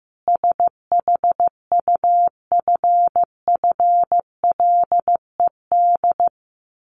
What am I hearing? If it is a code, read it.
Morse code, heard as a keyed tone, about 15 wpm, 700 Hz. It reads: SHUFFLED
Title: A Sound Of Thunder 15wpm